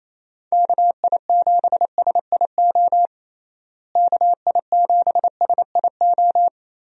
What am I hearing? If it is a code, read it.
Morse code, heard as a keyed tone, about 28 wpm, 700 Hz. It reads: KI7SIO KI7SIO
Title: USA Call Signs - 6 Characters - Repeated 2x - 28wpm